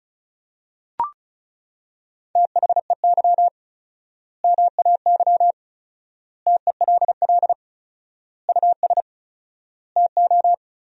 Morse code, heard as a keyed tone, about 34 wpm, 700 Hz. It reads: THEY MAY TELL US TO
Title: 4x Word-Spacing - Sentences from Top 300 Words - 35wpm